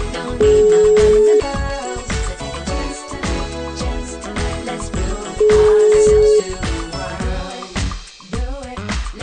Ringback tone (Europe, 425 Hz).